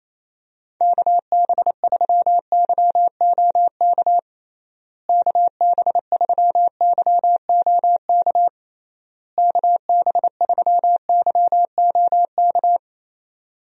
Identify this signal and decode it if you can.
Morse code, heard as a keyed tone, about 28 wpm, 700 Hz. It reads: KB3YOK KB3YOK KB3YOK
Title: USA Call Signs - 6 Characters - Repeated 3x - 28wpm